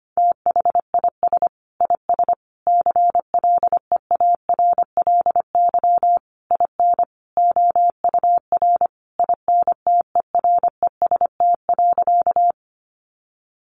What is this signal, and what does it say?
Morse code, heard as a keyed tone, about 25 wpm, 700 Hz. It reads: THIS IS CLEARLY IN OUR INTEREST.
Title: Sentences from Top 1300 Words 25wpm